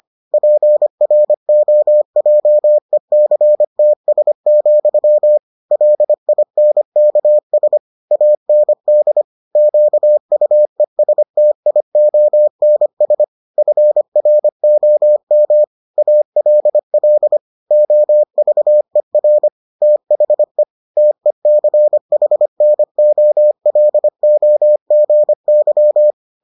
Morse code, heard as a keyed tone, about 25 wpm, 600 Hz. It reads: PROJECTS, LINKS AND QUESTIONS FROM ALL OVER THE TECHNOLOGY